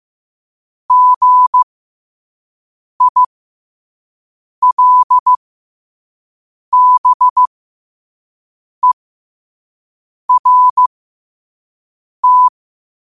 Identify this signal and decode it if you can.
Morse code, heard as a keyed tone, about 15 wpm, spaced as for 5 wpm, 1000 Hz. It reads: GILBERT
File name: qotdfiles / 5wpm